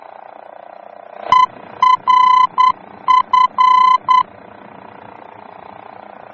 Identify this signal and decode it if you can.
Morse code, heard as a keyed tone, about 9 wpm, 1010 Hz. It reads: ERF
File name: germany / mx / ERF-D-VOR